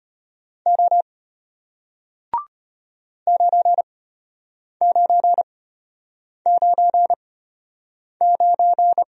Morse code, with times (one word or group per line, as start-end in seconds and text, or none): 0.7-1.0 O
3.3-9.0 9999